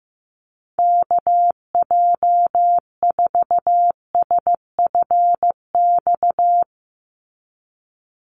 Morse code, heard as a keyed tone, about 15 wpm, 700 Hz. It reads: KJ4SFX
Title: USA Call Signs 15wpm